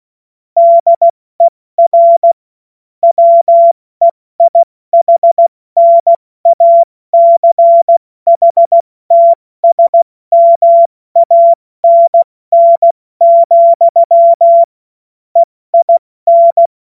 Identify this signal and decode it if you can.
Morse code, heard as a keyed tone, about 16 wpm, 680 Hz. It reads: DER WEIHNACHTSMANN, EIN
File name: Morsekurs / Witze / Witz 03. Januar 2023 70 F80